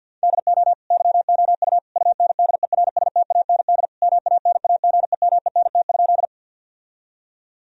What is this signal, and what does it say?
Morse code, heard as a keyed tone, about 50 wpm, 700 Hz. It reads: DO YOU UNDERSTAND MANAGEMENT?
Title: Sentences from Top 1000 Words - Review Part 4 - 50wpm